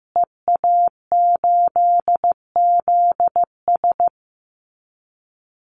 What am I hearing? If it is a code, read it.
Morse code, heard as a keyed tone, about 15 wpm, 700 Hz. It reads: EA8ZS
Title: Call Signs Repeated 2x - 15wpm